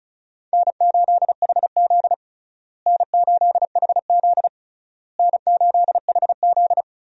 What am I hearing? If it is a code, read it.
Morse code, heard as a keyed tone, about 34 wpm, 700 Hz. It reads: N8HZ N8HZ N8HZ
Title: USA Call Signs - 4 Characters - Repeated 3x - 35wpm